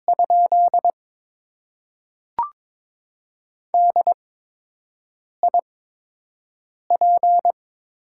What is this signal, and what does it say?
Morse code, heard as a keyed tone, about 22 wpm, spaced as for 6 wpm, 700 Hz. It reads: ? DIP
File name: Top 90 QSO Elements - 8x Character Spacing - 22wpm